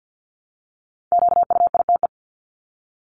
Morse code, heard as a keyed tone, about 50 wpm, 700 Hz. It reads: QUITE
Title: Top 501-750 Words 50wpm